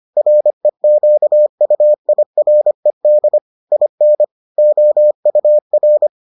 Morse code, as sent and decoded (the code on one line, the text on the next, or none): .-. --.- ..- .. .-. -.. / .. -. / --- ..- .-.
REQUIRED IN OUR